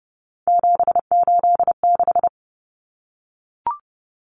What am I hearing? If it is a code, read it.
Morse code, heard as a keyed tone, about 30 wpm, 700 Hz. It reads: 786
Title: Common Cut Numbers - 3-Digit 30wpm